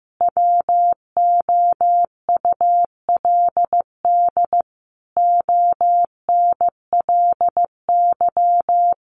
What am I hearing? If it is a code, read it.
Morse code, heard as a keyed tone, about 15 wpm, 700 Hz. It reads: WOULD ONLY
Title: Sentences from Top 900 Words 15wpm